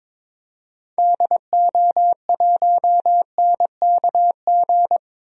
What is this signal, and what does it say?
Morse code, heard as a keyed tone, about 21 wpm, 700 Hz. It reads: DO1NKG
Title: DXCC-100 Call Signs - 6 Characters - 22wpm